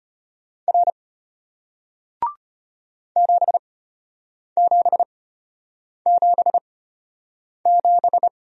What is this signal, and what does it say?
Morse code, heard as a keyed tone, about 34 wpm, spaced as for 8 wpm, 700 Hz. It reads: R 7777